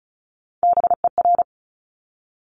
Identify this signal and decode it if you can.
Morse code, heard as a keyed tone, about 36 wpm, 700 Hz. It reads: BER
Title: Top Three Letter Combinations in Words 35wpm